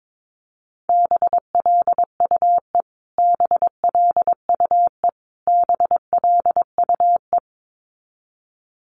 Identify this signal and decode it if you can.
Morse code, heard as a keyed tone, about 22 wpm, 700 Hz. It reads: BLUE BLUE BLUE
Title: Top 500 Words Repeated 22wpm